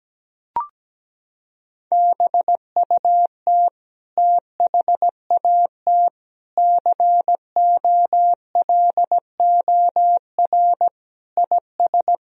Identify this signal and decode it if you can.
Morse code, heard as a keyed tone, about 17 wpm, 700 Hz. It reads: BUT THAT COLOR IS
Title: Sentences from Top 1000 Words - Review Part 2 - 17wpm